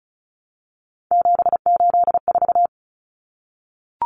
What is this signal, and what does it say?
Morse code, heard as a keyed tone, about 34 wpm, 700 Hz. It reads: 784
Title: Numbers - 3-Digit 35wpm